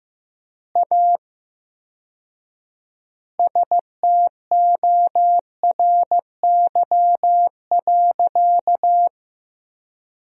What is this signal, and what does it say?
Morse code, heard as a keyed tone, about 15 wpm, 700 Hz. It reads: A STORY.
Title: 4x Word-Spacing - Sentences from Top 300 Words - 15wpm